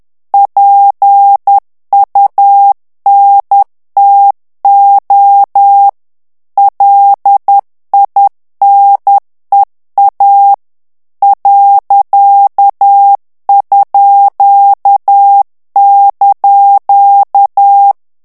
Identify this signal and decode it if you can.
Morse code, heard as a keyed tone, about 10 wpm, 800 Hz. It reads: PUNTO LINEA ._)
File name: morse mp3 / ITALIEN texte clair 1 vitesse 10 mots minute